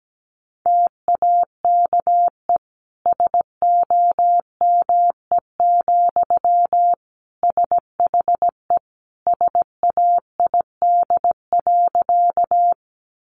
Morse code, heard as a keyed tone, about 17 wpm, 700 Hz. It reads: TAKE SOME, SHE SAID.
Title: Sentences from Top 200 Words 17wpm